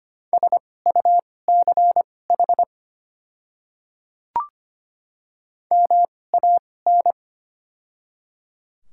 Morse code, heard as a keyed tone, about 25 wpm, spaced as for 18 wpm, 700 Hz. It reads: SUCH MAN
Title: 2x Character Spacing - Top 300 Words - 25wpm